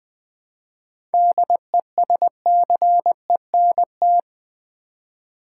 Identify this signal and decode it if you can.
Morse code, heard as a keyed tone, about 20 wpm, 700 Hz. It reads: DESCENT
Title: 7 Letter Words - First 3 Letters Given - 20wpm